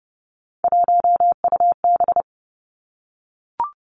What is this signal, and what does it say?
Morse code, heard as a keyed tone, about 31 wpm, 700 Hz. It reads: 1UB
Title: Three Challenging Letter-Number Combinations 30wpm